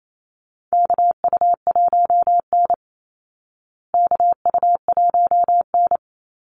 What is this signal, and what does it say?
Morse code, heard as a keyed tone, about 28 wpm, 700 Hz. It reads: KU1N KU1N
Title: Call Signs Repeated 2x - 28wpm